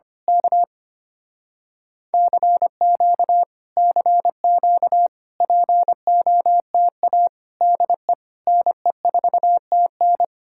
Morse code, heard as a keyed tone, about 24 wpm, 700 Hz. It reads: K CQ CQ POTA DE NE4TN
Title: Speed-Racing Common QSO Phrases - 25wpm